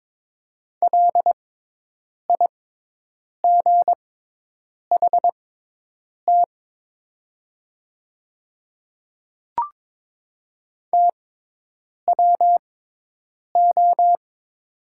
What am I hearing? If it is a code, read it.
Morse code, heard as a keyed tone, about 22 wpm, spaced as for 8 wpm, 700 Hz. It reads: LIGHT TWO
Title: 6x Character Spacing - Top 200 Words - 22wpm